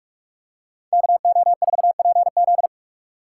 Morse code, heard as a keyed tone, about 44 wpm, 700 Hz. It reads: KO4PZ